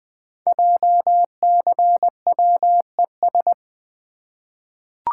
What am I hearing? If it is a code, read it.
Morse code, heard as a keyed tone, about 20 wpm, 700 Hz. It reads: JCWES
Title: Code Groups - 5 Letters - 20wpm